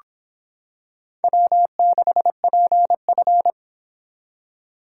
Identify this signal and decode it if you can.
Morse code, heard as a keyed tone, about 25 wpm, 700 Hz. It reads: W6PF